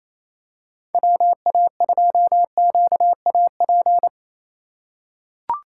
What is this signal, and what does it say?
Morse code, heard as a keyed tone, about 27 wpm, 700 Hz. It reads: WA2QAP
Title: USA Call Signs - 6 Characters - 6x Character Spacing - 28wpm